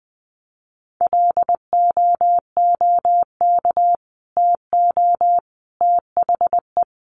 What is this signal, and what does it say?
Morse code, heard as a keyed tone, about 19 wpm, 700 Hz. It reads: LOOK TO THE
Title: Sentences from Top 400 Words 20wpm